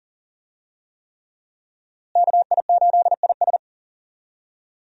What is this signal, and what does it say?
Morse code, heard as a keyed tone, about 40 wpm, 700 Hz. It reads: KI8IS
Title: USA Call Signs - 5 Characters - Repeated 3x - 40wpm